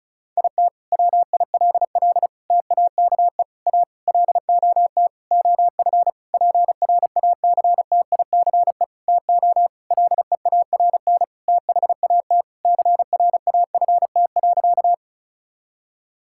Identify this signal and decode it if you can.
Morse code, heard as a keyed tone, about 36 wpm, 700 Hz. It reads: IT WILL TAKE A LOT OF PRACTICE TO LEARN THAT CRAFT.
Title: Sentences from Top 1000 Words - Review Part 6 - 35wpm